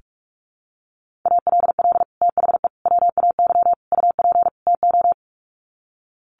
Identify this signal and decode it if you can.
Morse code, heard as a keyed tone, about 44 wpm, 700 Hz. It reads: ALL THE WAY UP TO
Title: Sets of 5 Words Encore 45wpm